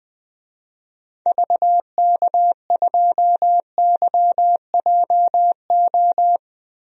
Morse code, heard as a keyed tone, about 20 wpm, 700 Hz. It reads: VK2YJO
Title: DXCC-100 Call Signs - 6 Characters - Repeated 2x - 20wpm